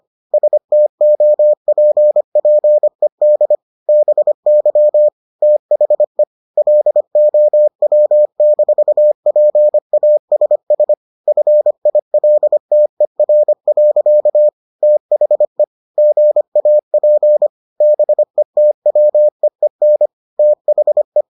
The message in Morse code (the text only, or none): STOPPED BY THE LOW-PASS FILTER. THE GAP BETWEEN THE